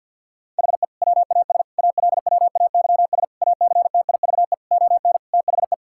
Morse code, heard as a keyed tone, about 51 wpm, 700 Hz. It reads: HE WAS ALWAYS ACTIVE ON THE
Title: Sentences from Top 1000 Words - Review Part 6 - 50wpm